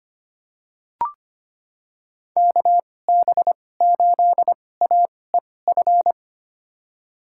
Morse code, heard as a keyed tone, about 25 wpm, spaced as for 18 wpm, 700 Hz. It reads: KB8AEF
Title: USA Call Signs - 6 Characters - 2x Character Spacing - 25wpm